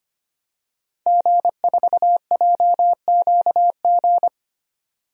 Morse code, heard as a keyed tone, about 25 wpm, 700 Hz. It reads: G4JQG
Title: DXCC-100 Call Signs - 5 Characters - 25wpm